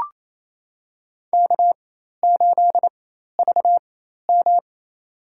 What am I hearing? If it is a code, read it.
Morse code, heard as a keyed tone, about 28 wpm, spaced as for 13 wpm, 700 Hz. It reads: K8VM
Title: USA Call Signs - 4 Characters - 4x Character Spacing - 28wpm